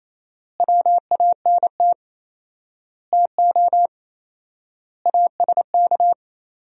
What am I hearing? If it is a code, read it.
Morse code, heard as a keyed tone, about 28 wpm, 700 Hz. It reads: WANT TO ASK
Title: 4x Word-Spacing - Sentences from Top 200 Words - 28wpm